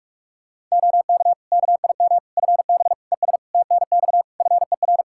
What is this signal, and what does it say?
Morse code, heard as a keyed tone, about 46 wpm, 700 Hz. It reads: OK KIM FB ES TNX FER
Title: Practice QSO - Exchange 2 - Info - 45wpm